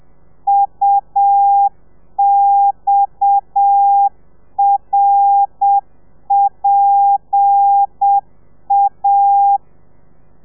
Morse code, heard as a keyed tone, about 7 wpm, 800 Hz. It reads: UXRPA
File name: telegrafia / i2tfj / Lez7 30